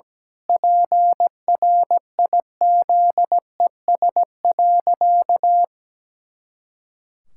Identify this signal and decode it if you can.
Morse code, heard as a keyed tone, about 17 wpm, 700 Hz. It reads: PRIZES.
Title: Pangram Challenge v2 - 17wpm